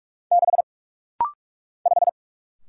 Morse code, heard as a keyed tone, about 44 wpm, 700 Hz. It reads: 6 5
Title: Single Letter-Number - Rapid-Fire 45wpm